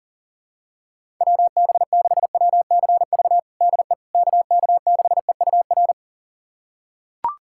Morse code, heard as a keyed tone, about 37 wpm, 700 Hz. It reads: WB6WCV DE KK6EUR
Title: USA Callsign DE Callsign - 6 Characters - 40wpm